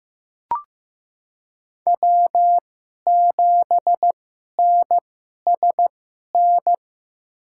Morse code, heard as a keyed tone, about 15 wpm, spaced as for 11 wpm, 700 Hz. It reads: W7NSN
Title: USA Call Signs - 5 Characters - 2x Character Spacing - 15wpm